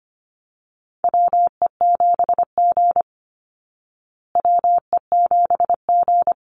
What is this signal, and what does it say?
Morse code, heard as a keyed tone, about 25 wpm, 700 Hz. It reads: WE7G WE7G
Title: Call Signs Repeated 3x - 25wpm